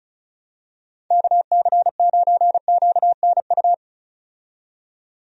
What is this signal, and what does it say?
Morse code, heard as a keyed tone, about 36 wpm, 700 Hz. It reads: KC9QNU
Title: USA Call Signs - 6 Characters - 35wpm